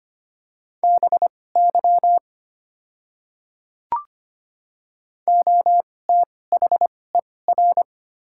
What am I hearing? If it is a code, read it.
Morse code, heard as a keyed tone, about 25 wpm, spaced as for 18 wpm, 700 Hz. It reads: BY OTHER
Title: 2x Character Spacing - Top 300 Words - 25wpm